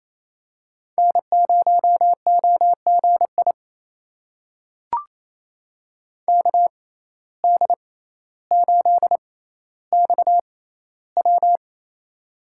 Morse code, heard as a keyed tone, about 28 wpm, spaced as for 14 wpm, 700 Hz. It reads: N0OGI KD8XW
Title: USA Call Signs - 5 Characters - 6x Character Spacing - 28wpm